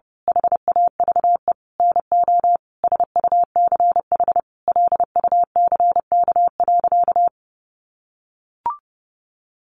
Morse code, heard as a keyed tone, about 30 wpm, 700 Hz. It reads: HAVE NO SUCH LUCK.
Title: Sentences from Top 1500 Words 30wpm